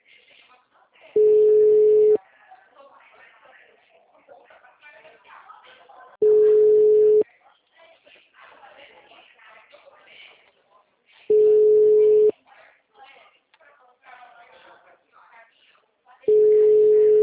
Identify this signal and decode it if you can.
Ringback tone (Europe, 425 Hz), with speech.